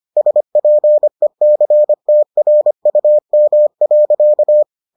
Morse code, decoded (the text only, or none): SPECTRUM.